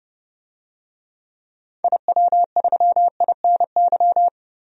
Morse code, heard as a keyed tone, about 30 wpm, 700 Hz. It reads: IW3INY
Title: DXCC-100 Call Signs - 6 Characters - 30wpm